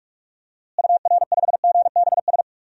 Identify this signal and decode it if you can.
Morse code, heard as a keyed tone, about 46 wpm, 700 Hz. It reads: UR5GBS
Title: DXCC-100 Call Signs - 6 Characters - Repeated 2x - 45wpm